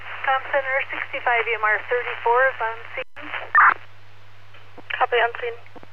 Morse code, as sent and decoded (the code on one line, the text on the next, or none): none